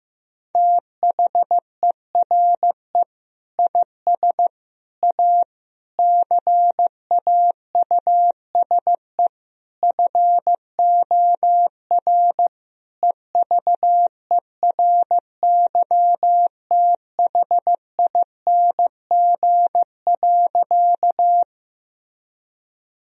Morse code, heard as a keyed tone, about 15 wpm, 700 Hz. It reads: THERE IS A CAUSE FOR EVERYTHING.
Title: Sentences from Top 1000 Words - Review Part 1 - 15wpm